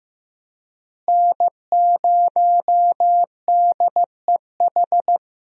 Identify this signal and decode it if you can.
Morse code, heard as a keyed tone, about 15 wpm, 700 Hz. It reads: N0DEH
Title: USA Call Signs - 5 Characters - Repeated 3x - 15wpm